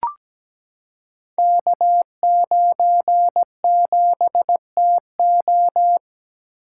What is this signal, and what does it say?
Morse code, heard as a keyed tone, about 17 wpm, 700 Hz. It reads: K97TO
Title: Code Groups - 5 Letter:Numbers - 17wpm